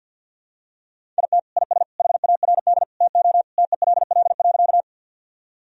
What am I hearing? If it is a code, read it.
Morse code, heard as a keyed tone, about 50 wpm, 700 Hz. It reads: IT IS HARD TO TELL.
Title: Sentences from Top 1000 Words - Review Part 1 - 50wpm